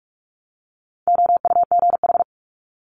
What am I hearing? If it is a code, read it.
Morse code, heard as a keyed tone, about 46 wpm, 700 Hz. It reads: OUGH
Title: Multi-letter Phonemes 45wpm